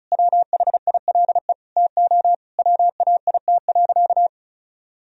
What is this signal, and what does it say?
Morse code, heard as a keyed tone, about 34 wpm, 700 Hz. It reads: WHILE TO WAIT.
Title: Sentences from Top 1000 Words - Review Part 1 - 35wpm